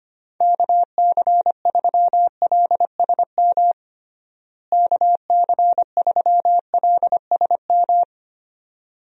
Morse code, heard as a keyed tone, about 25 wpm, 700 Hz. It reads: KC3LSM KC3LSM
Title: USA Call Signs - 6 Characters - Repeated 2x - 25wpm